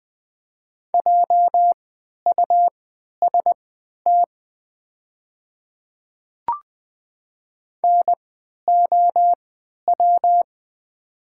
Morse code, heard as a keyed tone, about 20 wpm, spaced as for 11 wpm, 700 Hz. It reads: JUST NOW
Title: 3x Character Spacing - Top 300 Words - 20wpm